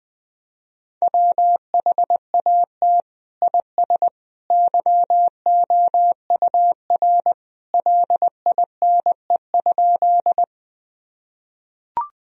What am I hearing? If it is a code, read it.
Morse code, heard as a keyed tone, about 20 wpm, 700 Hz. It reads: WHAT IS YOUR LINE?
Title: Sentences from Top 1000 Words - Review Part 1 - 20wpm